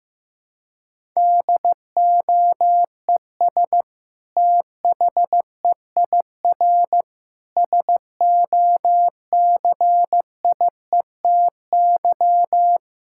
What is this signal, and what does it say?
Morse code, heard as a keyed tone, about 15 wpm, 700 Hz. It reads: DOES THEIR SOCIETY
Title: Sentences from Top 1000 Words - Review Part 6 - 15wpm